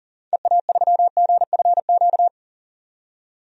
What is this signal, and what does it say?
Morse code, heard as a keyed tone, about 40 wpm, 700 Hz. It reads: EA3GFQ
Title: DXCC-100 Call Signs - 6 Characters - 40wpm